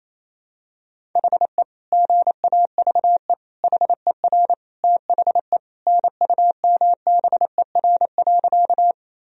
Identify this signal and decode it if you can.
Morse code, heard as a keyed tone, about 28 wpm, 700 Hz. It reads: HE GAVE HER THE NUMBER.
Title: Sentences from Top 1000 Words - Review Part 1 - 28wpm